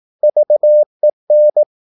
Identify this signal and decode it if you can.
Morse code, heard as a keyed tone, about 18 wpm, 600 Hz. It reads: VEN
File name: Korte ord DK 18wpm